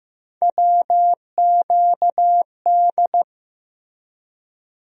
Morse code, heard as a keyed tone, about 15 wpm, 700 Hz. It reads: WQD
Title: USA Call Sign Postfixes - 3 Letters 15wpm